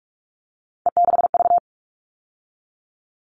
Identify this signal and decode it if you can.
Morse code, heard as a keyed tone, about 44 wpm, 700 Hz. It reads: E6V
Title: All Cut Numbers - 3-Digit 45wpm